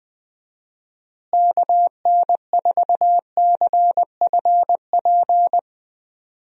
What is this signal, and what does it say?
Morse code, heard as a keyed tone, about 20 wpm, 700 Hz. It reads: KN4CFP
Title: USA Call Signs - 6 Characters - Repeated 3x - 20wpm